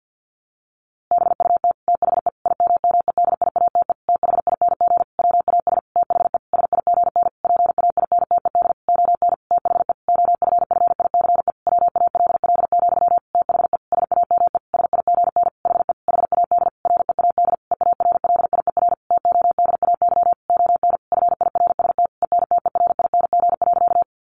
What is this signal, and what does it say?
Morse code, heard as a keyed tone, about 50 wpm, 700 Hz. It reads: BUT THE IMMEDIATE THING WAS THE SIGN PAINTED ON THE OFFICE WALL, THE SAME SIGN HE HAD READ EARLIER TODAY ON FIRST ENTERING.
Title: A Sound Of Thunder 50wpm